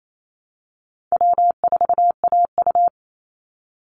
Morse code, heard as a keyed tone, about 28 wpm, 700 Hz. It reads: W4AU
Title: Easy Call Signs Repeated 3x - 28wpm